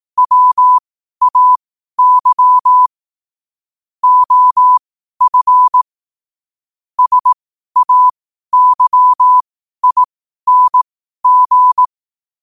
Morse code, heard as a keyed tone, about 18 wpm, spaced as for 12 wpm, 1000 Hz. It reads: WAY OF SAYING
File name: qotdfiles / 12wpm